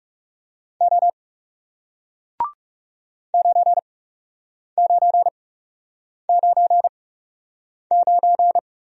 Morse code, heard as a keyed tone, about 40 wpm, spaced as for 8 wpm, 700 Hz. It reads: O 9999